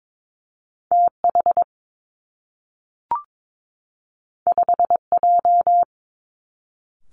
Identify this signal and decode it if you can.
Morse code, heard as a keyed tone, about 22 wpm, 700 Hz. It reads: T H 5 J
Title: Two Challenging Letter-Number Combinations 22wpm